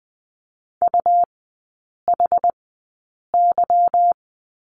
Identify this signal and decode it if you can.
Morse code, heard as a keyed tone, about 20 wpm, spaced as for 8 wpm, 700 Hz. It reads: UHY
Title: 198 Lesson 32e - Letter k - Three Characters 20wpm